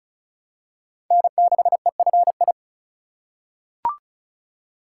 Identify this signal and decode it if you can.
Morse code, heard as a keyed tone, about 34 wpm, 700 Hz. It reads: N6EFI
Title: USA Call Signs - 5 Characters - 2x Character Spacing - 35wpm